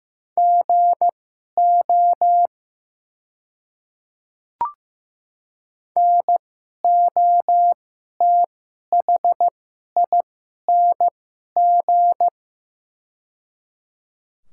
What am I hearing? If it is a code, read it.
Morse code, heard as a keyed tone, about 15 wpm, spaced as for 11 wpm, 700 Hz. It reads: GO NOTHING
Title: 2x Character Spacing - Top 500 Words - 15wpm